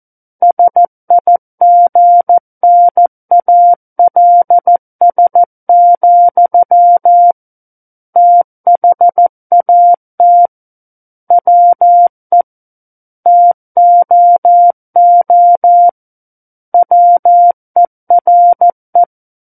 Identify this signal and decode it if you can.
Morse code, heard as a keyed tone, about 14 wpm, 700 Hz. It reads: SIGNALS, THAT WE TOO WERE
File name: Book One / 05-Chapter 5 - 14 WPM